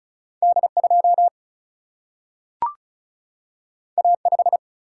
Morse code, heard as a keyed tone, about 36 wpm, 700 Hz. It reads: D2 A5
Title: Single-Letter Single-Number 35wpm